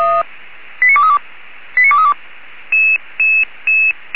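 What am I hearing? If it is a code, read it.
Five-tone selective call (ZVEI), heard as quick runs of five tones, tones of about 70 ms. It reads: 89232 89232
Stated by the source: SDR-received ZVEI five-tone selective call.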